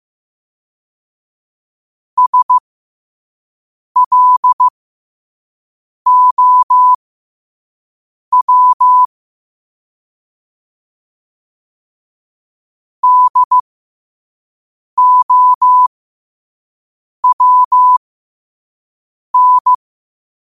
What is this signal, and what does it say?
Morse code, heard as a keyed tone, about 15 wpm, spaced as for 5 wpm, 1000 Hz. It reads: SLOW DOWN
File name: qotdfiles / 5wpm